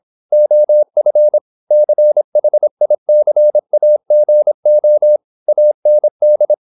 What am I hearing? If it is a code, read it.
Morse code, heard as a keyed tone, about 26 wpm, 600 Hz. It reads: OF CHICAGO AND
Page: Never 26wpm | CW med Gnister